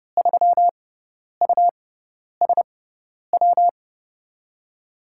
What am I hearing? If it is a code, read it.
Morse code, heard as a keyed tone, about 31 wpm, spaced as for 10 wpm, 700 Hz. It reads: 3USW
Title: USA Call Signs - 6 Characters - 6x Character Spacing - 30wpm